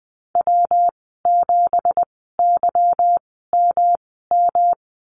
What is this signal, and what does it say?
Morse code, heard as a keyed tone, about 20 wpm, spaced as for 14 wpm, 700 Hz. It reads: W7YMM
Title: 193 Lesson 31g - Letter v - Call Signs 20wpm